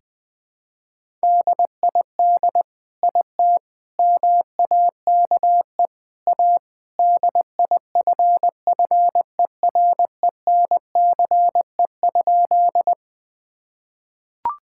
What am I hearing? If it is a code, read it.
Morse code, heard as a keyed tone, about 21 wpm, 700 Hz. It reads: DID IT MAKE A DIFFERENCE?
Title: Sentences from Top 1000 Words - Review Part 5 - 20wpm